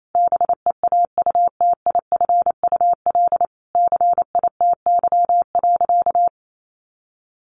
Morse code, heard as a keyed tone, about 28 wpm, 700 Hz. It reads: BEAUTIFUL CITY.
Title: Sentences from Top 1100 Words 28wpm